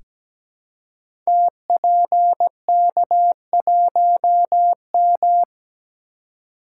Morse code, heard as a keyed tone, about 17 wpm, 700 Hz. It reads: TPK1M
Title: Code Groups - 5 Letter:Numbers - 17wpm